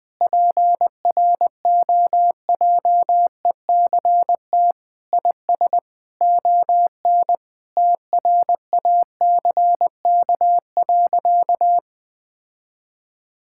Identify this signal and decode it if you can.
Morse code, heard as a keyed tone, about 20 wpm, 700 Hz. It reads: PROJECT IS ON TRACK.
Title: Sentences with 6-Word Phrases - 20wpm